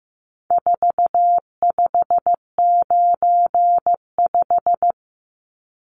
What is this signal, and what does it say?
Morse code, heard as a keyed tone, about 15 wpm, 700 Hz. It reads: 4595
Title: Numbers - 4-Digit 15wpm